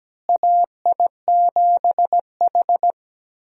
Morse code, heard as a keyed tone, about 17 wpm, 700 Hz. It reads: AI7H
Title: USA Call Signs - 4 Characters - Repeated 2x - 17wpm